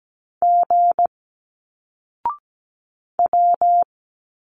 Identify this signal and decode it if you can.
Morse code, heard as a keyed tone, about 17 wpm, 700 Hz. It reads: G W
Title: Single Letters 17wpm